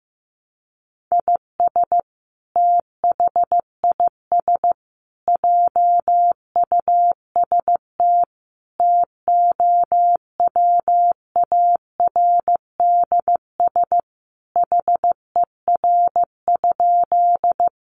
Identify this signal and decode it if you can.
Morse code, heard as a keyed tone, about 15 wpm, 700 Hz. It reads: IS THIS JUST TOWARDS HER?
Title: Sentences from Top 1100 Words 15wpm